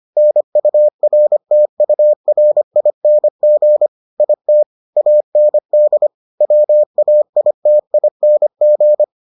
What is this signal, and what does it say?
Morse code, heard as a keyed tone, about 25 wpm, 600 Hz. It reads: NURTURING IT AND WAITING